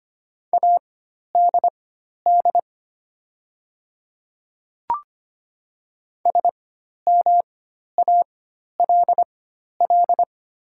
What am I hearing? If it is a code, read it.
Morse code, heard as a keyed tone, about 25 wpm, spaced as for 12 wpm, 690 Hz. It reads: ADD SMALL
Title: 4x Character Spacing - Top 200 Words - 25wpm